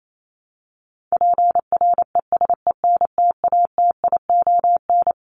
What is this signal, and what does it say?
Morse code, heard as a keyed tone, about 28 wpm, 700 Hz. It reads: PRESENTATION
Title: Top 1501-1750 Words 28wpm